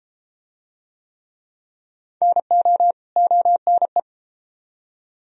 Morse code, heard as a keyed tone, about 32 wpm, 700 Hz. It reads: NO ONE